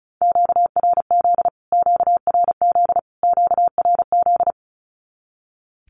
Morse code, heard as a keyed tone, about 34 wpm, 700 Hz. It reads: QRZ QRZ QRZ
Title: Q-Codes Repeated 3x 35wpm